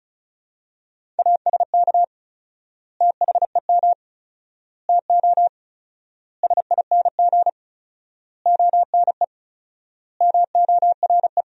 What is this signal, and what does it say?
Morse code, heard as a keyed tone, about 36 wpm, 700 Hz. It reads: ASK THEM TO SING ONE MORE
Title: 4x Word-Spacing - Sentences from Top 400 Words - 35wpm